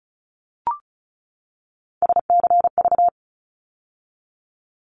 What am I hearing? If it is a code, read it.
Morse code, heard as a keyed tone, about 34 wpm, 700 Hz. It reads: SCV
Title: ARRL Sections 35wpm